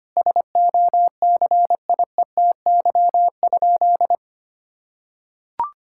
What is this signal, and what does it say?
Morse code, heard as a keyed tone, about 25 wpm, 700 Hz. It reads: SOCIETY?
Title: Sentences from Top 1000 Words - Review Part 6 - 25wpm